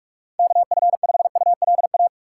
Morse code, heard as a keyed tone, about 43 wpm, 700 Hz. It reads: KF5ULA
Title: USA Call Signs - 6 Characters - 2x Character Spacing - 45wpm